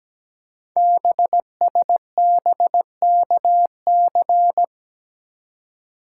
Morse code, heard as a keyed tone, about 17 wpm, 700 Hz. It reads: BSBKC
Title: Code Groups - 5 Letters - 17wpm